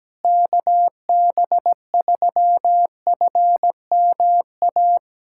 Morse code, heard as a keyed tone, about 17 wpm, 700 Hz. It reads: KB3FMA
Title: USA Callsign DE Callsign - 6 Characters - 17wpm